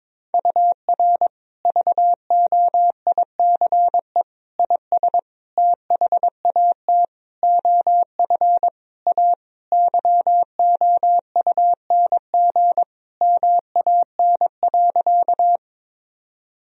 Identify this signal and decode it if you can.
Morse code, heard as a keyed tone, about 22 wpm, 700 Hz. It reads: UR VOICE IS THAT OF A YOUNG MAN.
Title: Sentences from Top 500 Words and Abbreviations 22wpm